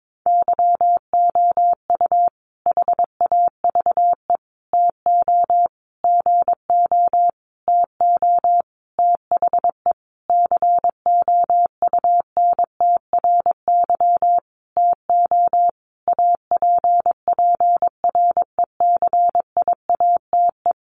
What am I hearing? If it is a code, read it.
Morse code, heard as a keyed tone, about 22 wpm, 700 Hz. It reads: YOU HAVE TO GO TO THE COUNTRY TO APPRECIATE
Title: Sentences from Top 1300 Words 22wpm